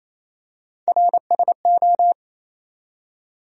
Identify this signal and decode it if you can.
Morse code, heard as a keyed tone, about 28 wpm, 700 Hz. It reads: RSO
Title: USA Call Sign Postfixes - 3 Letters 28wpm